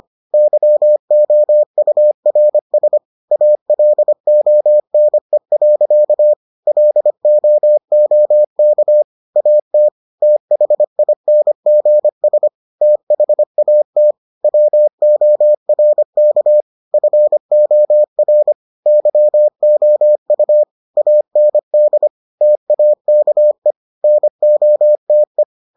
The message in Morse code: YOURS ALONE. LOOK AT THINGS THAT WORK FOR YOU AND TAKE NOTE

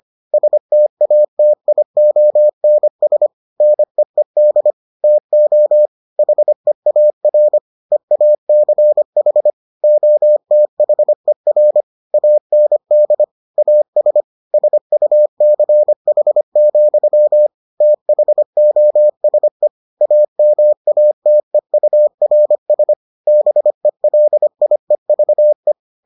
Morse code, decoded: STATIONS NEED TO HEAR EACH OTHER AND AS SUCH, THOSE AMATEURS BELIEVE

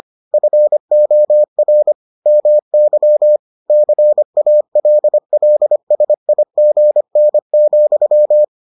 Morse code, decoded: FOR MY CALLSIGN,

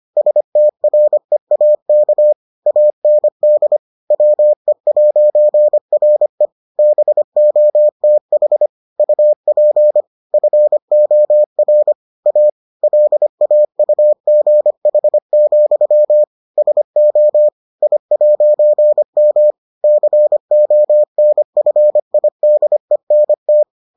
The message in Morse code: STREAK AND WE'RE BOTH UP FOR A LAUGH, SO I'M CONFIDENT

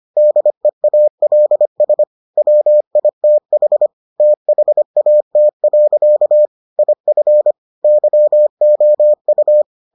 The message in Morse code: DEALS WITH THAT. IF YOU

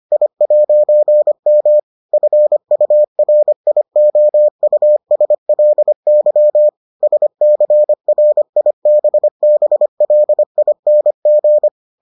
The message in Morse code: I'M FURIOUSLY SCRIBBLING

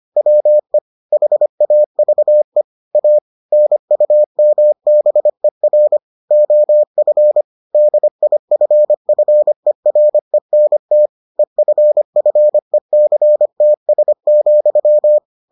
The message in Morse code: WE HAVE A NUMBER OF DIFFERENT EFFECTS,

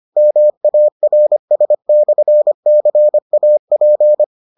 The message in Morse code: MARS/CAP

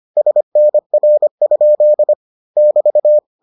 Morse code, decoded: SNR? <BT>

